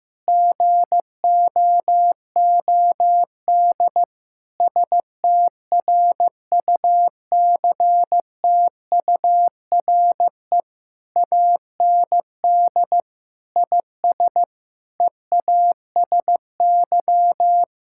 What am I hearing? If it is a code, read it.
Morse code, heard as a keyed tone, about 15 wpm, 700 Hz. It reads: GOOD STRUCTURE AND IS EASY
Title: Sentences from Top 1000 Words - Review Part 6 - 15wpm